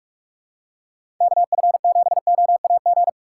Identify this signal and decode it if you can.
Morse code, heard as a keyed tone, about 46 wpm, 700 Hz. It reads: KF7OAG
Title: USA Call Signs - 6 Characters - 6x Character Spacing - 45wpm